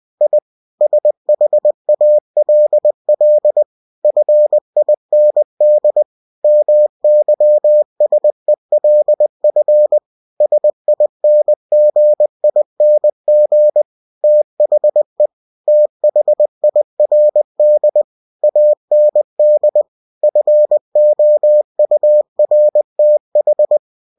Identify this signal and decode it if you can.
Morse code, heard as a keyed tone, about 20 wpm, 600 Hz. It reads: I SHALL FIND MYSELF SINGING THE THIRD AND FOURTH